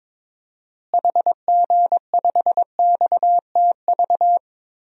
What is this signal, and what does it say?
Morse code, heard as a keyed tone, about 22 wpm, 700 Hz. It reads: HG5XTV
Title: DXCC-100 Call Signs - 6 Characters - Repeated 2x - 22wpm